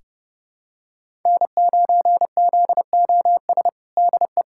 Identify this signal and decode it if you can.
Morse code, heard as a keyed tone, about 30 wpm, 700 Hz. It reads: N9ZOS DE
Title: USA Callsign DE Callsign - 5 Characters - 30wpm